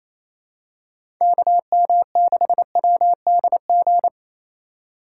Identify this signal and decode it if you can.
Morse code, heard as a keyed tone, about 28 wpm, 700 Hz. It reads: KM6WDG
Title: USA Call Signs - 6 Characters - 8x Character Spacing - 28wpm